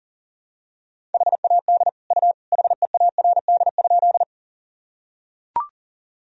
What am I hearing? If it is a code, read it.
Morse code, heard as a keyed tone, about 40 wpm, 700 Hz. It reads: HAD U HEARD?
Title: Sentences from Top 400 Words and Abbreviations 40wpm